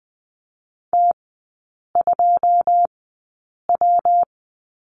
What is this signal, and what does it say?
Morse code, heard as a keyed tone, about 20 wpm, spaced as for 8 wpm, 700 Hz. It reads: T2W
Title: 218 Lesson 35e - Number 0 - Three Characters 20wpm